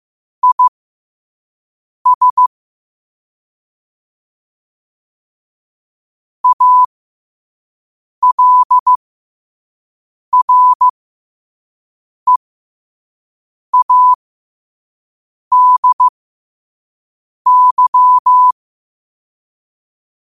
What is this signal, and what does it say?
Morse code, heard as a keyed tone, about 15 wpm, spaced as for 5 wpm, 1000 Hz. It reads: IS ALREADY
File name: qotdfiles / 5wpm